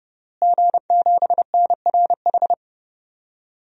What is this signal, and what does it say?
Morse code, heard as a keyed tone, about 32 wpm, 700 Hz. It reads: G7NRH
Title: DXCC-100 Call Signs - 5 Characters - Repeated 3x - 30wpm